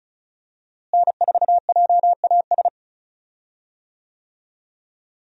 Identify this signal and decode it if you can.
Morse code, heard as a keyed tone, about 36 wpm, 700 Hz. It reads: N4JAS
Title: USA Call Signs - 5 Characters - Repeated 3x - 35wpm